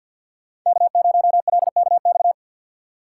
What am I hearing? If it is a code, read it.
Morse code, heard as a keyed tone, about 52 wpm, 700 Hz. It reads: K0LKX
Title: USA Call Signs - 5 Characters - 3x Character Spacing - 50wpm